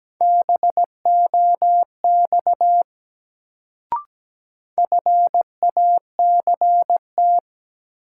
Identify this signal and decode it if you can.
Morse code, heard as a keyed tone, about 17 wpm, 700 Hz. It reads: BOX FACT
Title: Sentences from Top 500 Words - New Word Drill - 17wpm